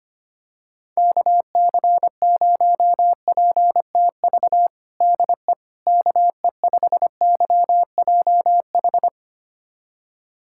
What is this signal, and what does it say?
Morse code, heard as a keyed tone, about 25 wpm, 700 Hz. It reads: KC0PTV DE KE5YJH
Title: USA Callsign DE Callsign 25wpm